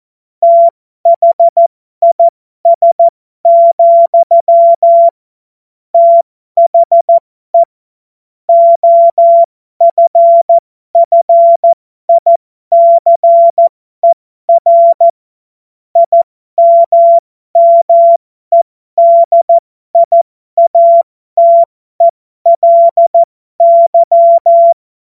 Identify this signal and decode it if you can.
Morse code, heard as a keyed tone, about 14 wpm, spaced as for 12 wpm, 680 Hz. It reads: THIS, THE OFFICER IMMEDIATELY
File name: Morsekurs / Witze